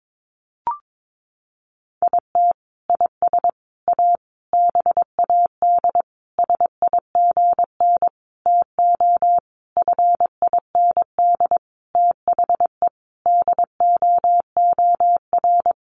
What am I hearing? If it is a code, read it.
Morse code, heard as a keyed tone, about 22 wpm, 700 Hz. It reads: IT IS A BAD SIGN TO FIND THE DOOR
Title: Sentences from Top 1000 Words 22wpm